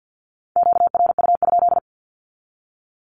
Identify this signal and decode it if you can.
Morse code, heard as a keyed tone, about 50 wpm, 700 Hz. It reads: QRU?
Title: Q-Codes with Spoken Meaning - 50wpm